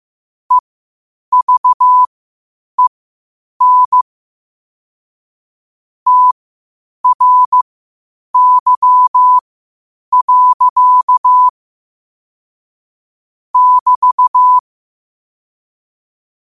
Morse code, heard as a keyed tone, about 15 wpm, spaced as for 8 wpm, 1000 Hz. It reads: EVEN TRY. <BT>
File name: qotdfiles / 8wpm